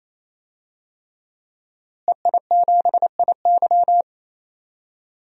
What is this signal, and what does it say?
Morse code, heard as a keyed tone, about 28 wpm, 700 Hz. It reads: EI7IY
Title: DXCC-100 Call Signs - 5 Characters - Repeated 3x - 28wpm